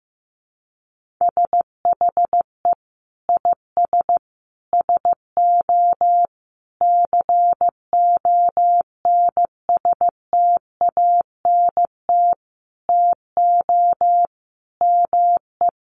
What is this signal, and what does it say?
Morse code, heard as a keyed tone, about 15 wpm, 700 Hz. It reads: SHE IS SO CONSTANT TO ME
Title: Sentences from Top 1500 Words 15wpm